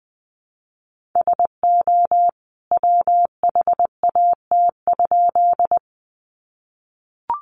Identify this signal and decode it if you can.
Morse code, heard as a keyed tone, about 20 wpm, 700 Hz. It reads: SO WHAT?
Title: A Sound Of Thunder 20wpm